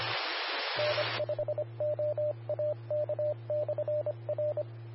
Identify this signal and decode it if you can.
Morse code, heard as a keyed tone, about 25 wpm, 600 Hz, with radio noise. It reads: N5OAK/R
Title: Austin, TX Amateur Repeaters